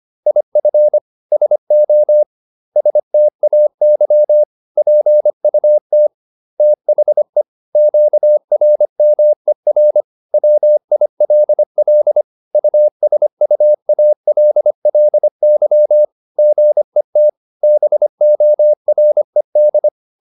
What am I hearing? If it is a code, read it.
Morse code, heard as a keyed tone, about 25 wpm, 600 Hz. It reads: IF SO STAY PUT THE QRMER WILL USUALLY GET BORED